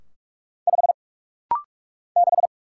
Morse code, heard as a keyed tone, about 44 wpm, 700 Hz. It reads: 5 6
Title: Single Letter-Number - Rapid-Fire 45wpm